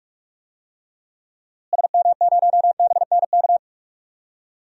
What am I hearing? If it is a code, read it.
Morse code, heard as a keyed tone, about 44 wpm, 700 Hz. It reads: SM0BNK
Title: DXCC-100 Call Signs - 6 Characters - Repeated 3x - 45wpm